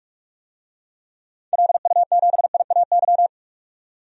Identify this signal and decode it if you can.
Morse code, heard as a keyed tone, about 45 wpm, 700 Hz. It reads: LU7IAY
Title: DXCC-100 Call Signs - 6 Characters - 45wpm